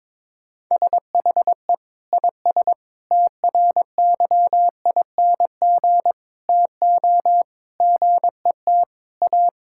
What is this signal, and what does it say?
Morse code, heard as a keyed tone, about 22 wpm, 700 Hz. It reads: SHE IS TRYING TO GET A
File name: practice / 22wpm / Sentences with 3-Word Phrases - 22wpm